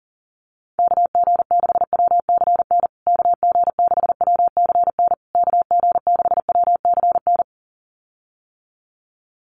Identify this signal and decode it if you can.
Morse code, heard as a keyed tone, about 40 wpm, 700 Hz. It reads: KG6WCN KG6WCN KG6WCN
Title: USA Call Signs Repeated 3x 40wpm